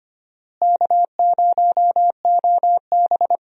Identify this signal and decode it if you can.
Morse code, heard as a keyed tone, about 25 wpm, 700 Hz. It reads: K0OB
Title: USA Call Signs - 4 Characters - Repeated 3x - 25wpm